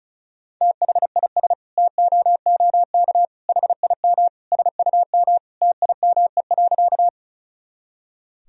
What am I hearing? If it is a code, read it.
Morse code, heard as a keyed tone, about 34 wpm, 700 Hz. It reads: THIS TOOK HIM SUM TIME.
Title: Sentences from Top 300 Words and Abbreviations 35wpm